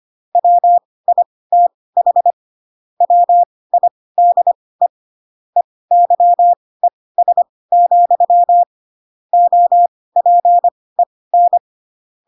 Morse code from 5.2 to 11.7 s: EYES, OPEN